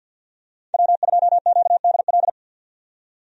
Morse code, heard as a keyed tone, about 51 wpm, 700 Hz. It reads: W2QDL
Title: USA Call Signs - 5 Characters - 50wpm